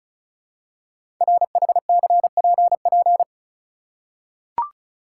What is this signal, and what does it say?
Morse code, heard as a keyed tone, about 36 wpm, 690 Hz. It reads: RHCPP
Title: Code Groups - 5 Letters - 35wpm